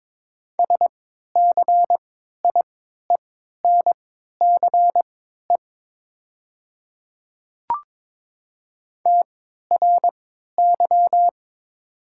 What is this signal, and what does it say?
Morse code, heard as a keyed tone, about 21 wpm, spaced as for 12 wpm, 690 Hz. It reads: SCIENCE TRY
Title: 3x Character Spacing - Top 500 Words - 22wpm